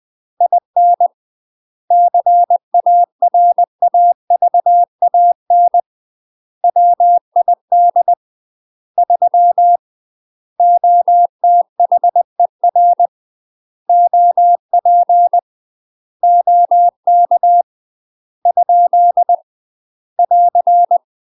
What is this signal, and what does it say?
Morse code, heard as a keyed tone, about 20 wpm, 700 Hz. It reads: IN CARAVAN WID 3 OTHER OP OK ? <AR>
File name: UK MASTER QSOs TRAINING / 20WPM / UK MASTER QSOs 0073